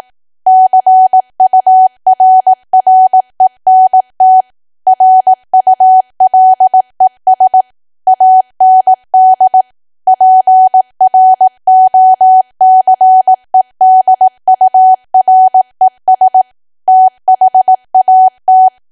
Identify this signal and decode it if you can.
Morse code, heard as a keyed tone, about 18 wpm, 750 Hz. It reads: CURRENT RULES AND PROCEDURES THAT
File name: cwa / W1AW-Level2 / 18WPM